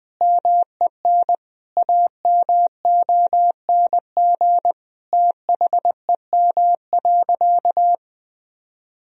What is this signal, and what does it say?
Morse code, heard as a keyed tone, about 20 wpm, 700 Hz. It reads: MEN AMONG THEM.
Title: Sentences from Top 1000 Words - Review Part 2 - 20wpm